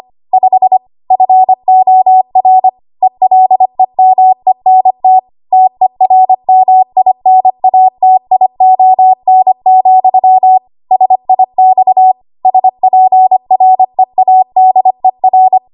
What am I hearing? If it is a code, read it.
Morse code, heard as a keyed tone, about 25 wpm, 750 Hz. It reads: 5 FOR ELEMENT TERMINATION, SIX SPREADER
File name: Archive / 25 WPM / 25WPM